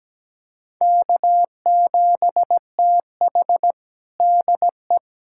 Morse code, heard as a keyed tone, about 17 wpm, 700 Hz. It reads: K7TH DE
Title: USA Callsign DE Callsign - 4 Characters - 17wpm